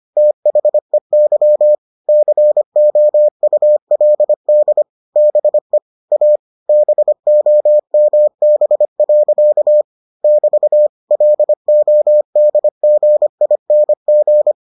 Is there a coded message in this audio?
Yes: Morse code from 0.2 to 14.5 s, heard as a keyed tone, about 25 wpm, 600 Hz. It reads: THEY COULD BE A BOMB. <BT> LODGING